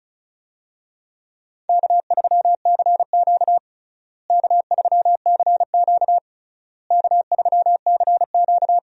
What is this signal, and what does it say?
Morse code, heard as a keyed tone, about 36 wpm, 700 Hz. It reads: K3CQ K3CQ K3CQ
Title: USA Call Signs - 4 Characters - Repeated 3x - 35wpm